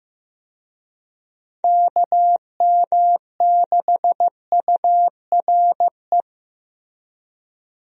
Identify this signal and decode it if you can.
Morse code, heard as a keyed tone, about 15 wpm, 700 Hz. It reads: KM6URE
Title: USA Call Signs - 6 Characters - Repeated 3x - 15wpm